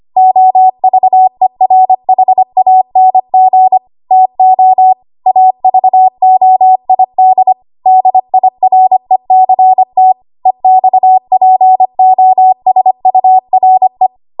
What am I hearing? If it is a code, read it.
Morse code, heard as a keyed tone, about 25 wpm, 750 Hz. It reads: OVERHANG TO AVOID DIRECT EXPOSURE
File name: Archive / 25 WPM / 25WPM